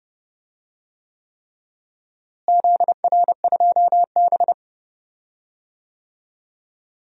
Morse code, heard as a keyed tone, about 30 wpm, 700 Hz. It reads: ZR2B
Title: DXCC-100 Call Signs - 4 Characters - Repeated 3x - 30wpm